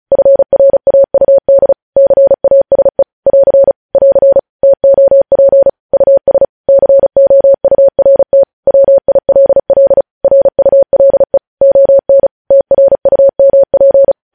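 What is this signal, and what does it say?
Morse code, heard as a keyed tone, about 36 wpm, 550 Hz. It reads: FRAUD CASE <AR> <AR> TOP US COURT WILL RULE ON TRUMP
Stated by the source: News headlines sent in Morse code at 35 WPM.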